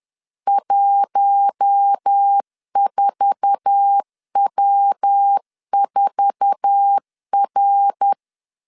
Morse code, heard as a keyed tone, about 10 wpm, 800 Hz. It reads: 14W4R
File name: LEZIONI / 8 YJ45 / 8 YJ4 10wpm 2633